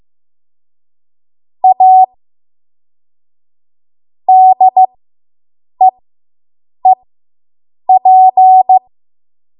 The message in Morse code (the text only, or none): A DEEP